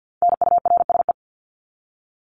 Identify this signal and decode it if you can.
Morse code, heard as a keyed tone, about 48 wpm, 700 Hz. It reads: NURSE
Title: Top 1751-2000 Words 50wpm